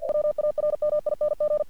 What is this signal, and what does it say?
Morse code, heard as a keyed tone, about 48 wpm, 600 Hz. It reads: WARMING
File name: archived / CRSnet